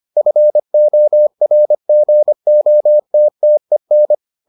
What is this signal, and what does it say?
Morse code, heard as a keyed tone, about 25 wpm, 600 Hz. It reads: FORGOTTEN